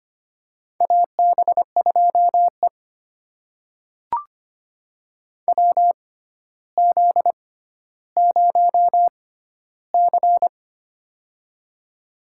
Morse code, heard as a keyed tone, about 25 wpm, spaced as for 13 wpm, 700 Hz. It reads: AB2E WZ0C
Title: USA Call Signs - 4 Characters - 6x Character Spacing - 25wpm